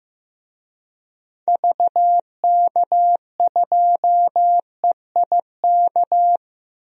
Morse code, heard as a keyed tone, about 15 wpm, 700 Hz. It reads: VK2EIK
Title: DXCC-100 Call Signs - 6 Characters - 15wpm